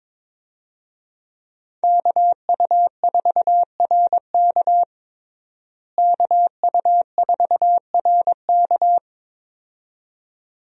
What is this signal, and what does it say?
Morse code, heard as a keyed tone, about 22 wpm, 700 Hz. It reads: KU4RK KU4RK
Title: USA Call Signs - 5 Characters - Repeated 2x - 22wpm